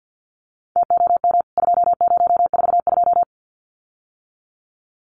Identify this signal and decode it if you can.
Morse code, heard as a keyed tone, about 52 wpm, 700 Hz. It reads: TOM 2042
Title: CWT Exchange 50wpm